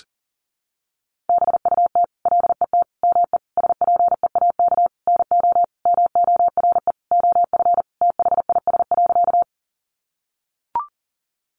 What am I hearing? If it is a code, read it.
Morse code, heard as a keyed tone, about 40 wpm, 700 Hz. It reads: BUT LET ME SPEAK NO MORE OF THIS.
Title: Sentences from Top 1100 Words 40wpm